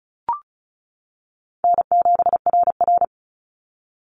Morse code, heard as a keyed tone, about 34 wpm, 700 Hz. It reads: N7RR
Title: NA SOTA Activator Callsigns 35wpm